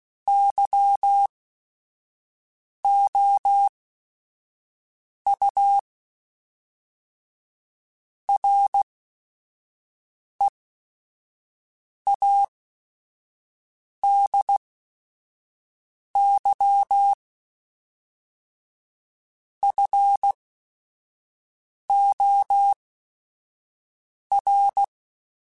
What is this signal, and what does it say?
Morse code, heard as a keyed tone, about 16 wpm, spaced as for 5 wpm, 800 Hz. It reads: YOU READY FOR
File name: Technical Subjects / Technical Subjects-mp3s / OR Gate @5